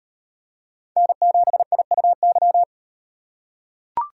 Morse code, heard as a keyed tone, about 36 wpm, 700 Hz. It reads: N7IUY